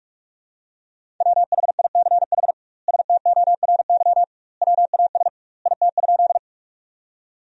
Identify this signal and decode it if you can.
Morse code, heard as a keyed tone, about 43 wpm, 700 Hz. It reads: WHICH STORY WAS IT?
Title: 2x Word-Spacing - Sentences from Top 300 Words - 45wpm